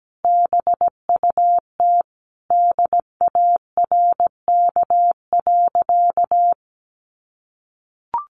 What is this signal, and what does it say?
Morse code, heard as a keyed tone, about 17 wpm, 700 Hz. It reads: BUT DARK.
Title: Sentences from Top 1400 Words 17wpm